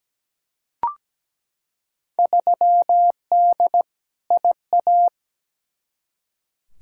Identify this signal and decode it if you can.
Morse code, heard as a keyed tone, about 17 wpm, 700 Hz. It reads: 3D IA
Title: Field Day Exchanges Encore - 17wpm